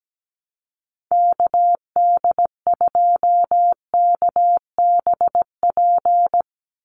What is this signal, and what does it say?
Morse code, heard as a keyed tone, about 17 wpm, 700 Hz. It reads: KD2KBP
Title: USA Call Signs 17wpm